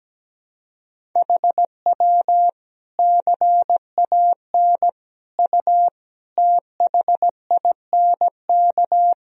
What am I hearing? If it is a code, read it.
Morse code, heard as a keyed tone, about 17 wpm, 700 Hz. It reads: HW CAN U THINK
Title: Sentences from Top 200 Words and Abbreviations 17wpm